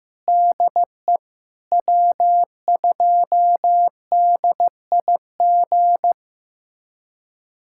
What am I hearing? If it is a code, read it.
Morse code, heard as a keyed tone, about 15 wpm, 700 Hz. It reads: DE W2DIG
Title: USA Callsign DE Callsign - 5 Characters - 15wpm